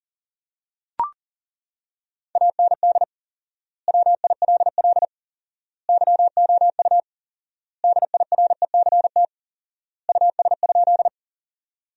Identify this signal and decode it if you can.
Morse code, heard as a keyed tone, about 40 wpm, 700 Hz. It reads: AND WILL YOU DIRECT US?
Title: 4x Word-Spacing - Sentences from Top 400 Words - 40wpm